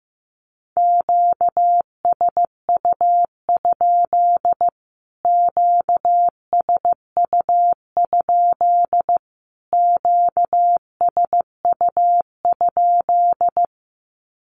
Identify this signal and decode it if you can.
Morse code, heard as a keyed tone, about 15 wpm, 700 Hz. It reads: QSU? QSU? QSU?
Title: Q-Codes Repeated 3x 15wpm